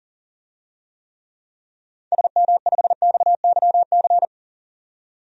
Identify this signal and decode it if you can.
Morse code, heard as a keyed tone, about 40 wpm, 700 Hz. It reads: SM5XYC
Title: DXCC-100 Call Signs - 6 Characters - 40wpm